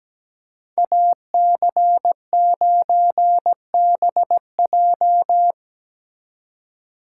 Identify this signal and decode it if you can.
Morse code, heard as a keyed tone, about 17 wpm, 700 Hz. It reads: AC9BJ
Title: USA Call Signs - 5 Characters - 2x Character Spacing - 17wpm